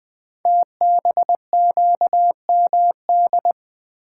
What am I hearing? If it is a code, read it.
Morse code, heard as a keyed tone, about 20 wpm, 700 Hz. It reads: TBQMD
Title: Code Groups - 5 Letters - 20wpm